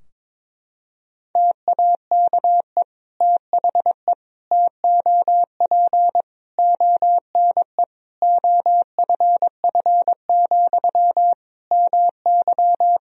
Morse code, heard as a keyed tone, about 22 wpm, 700 Hz. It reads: TAKE THE TOP ONE OFF, MY
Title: Sentences from Top 1000 Words - Review Part 2 - 22wpm